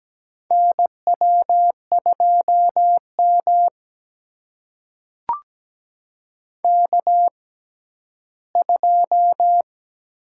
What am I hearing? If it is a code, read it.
Morse code, heard as a keyed tone, about 17 wpm, spaced as for 11 wpm, 700 Hz. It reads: NW2M K2
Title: USA Call Signs - 4 Characters - 6x Character Spacing - 17wpm